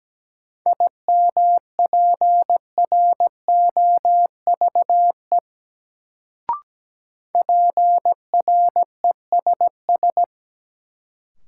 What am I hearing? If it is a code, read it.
Morse code, heard as a keyed tone, about 17 wpm, 700 Hz. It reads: IMPROVE PRESS
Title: Sentences from Top 800 Words - New Word Drill - 17wpm